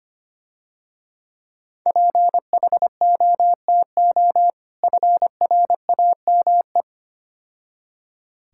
Morse code, heard as a keyed tone, about 27 wpm, 700 Hz. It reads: PHOTO FRAME
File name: Difficult Words - Speed-Racing with Hints - 25wpm